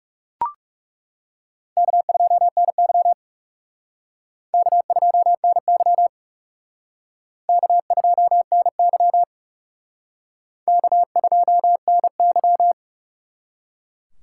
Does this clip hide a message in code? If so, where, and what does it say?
Yes: Morse code from 1.8 to 12.7 s, heard as a keyed tone, about 38 wpm, 700 Hz. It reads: K2NY K2NY K2NY K2NY